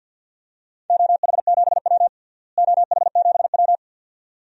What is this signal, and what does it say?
Morse code, heard as a keyed tone, about 52 wpm, 700 Hz. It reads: OH7W OH7W
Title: DXCC-100 Call Signs - 4 Characters - Repeated 2x - 50wpm